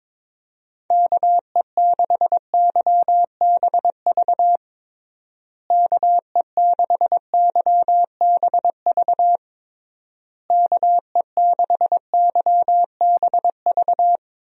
Morse code, heard as a keyed tone, about 22 wpm, 700 Hz. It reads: KE6YBV KE6YBV KE6YBV
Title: USA Call Signs - 6 Characters - Repeated 3x - 22wpm